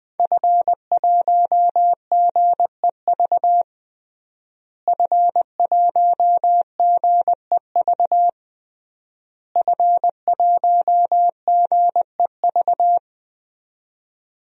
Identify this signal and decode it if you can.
Morse code, heard as a keyed tone, about 20 wpm, 700 Hz. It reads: F1GEV F1GEV F1GEV
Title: DXCC-100 Call Signs - 5 Characters - Repeated 3x - 20wpm